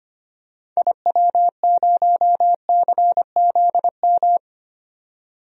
Morse code, heard as a keyed tone, about 25 wpm, 700 Hz. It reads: IW0CZM
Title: DXCC-100 Call Signs - 6 Characters - 25wpm